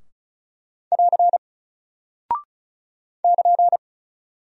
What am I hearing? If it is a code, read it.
Morse code, heard as a keyed tone, about 36 wpm, 700 Hz. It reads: <AR> <KN>
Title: Single Character (Letter:Number:Punctuation:Prosign) Alt 35wpm